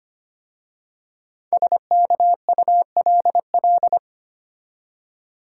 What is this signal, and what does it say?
Morse code, heard as a keyed tone, about 25 wpm, 700 Hz. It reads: SKULL
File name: 5 Letter Words - First 3 Letters Given - 25wpm